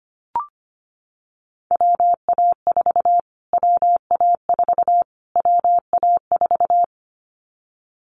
Morse code, heard as a keyed tone, about 25 wpm, 700 Hz. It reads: WA4 WA4 WA4
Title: Callsign Prefixes for Top 25 DXCC - Repeated 3x - 25wpm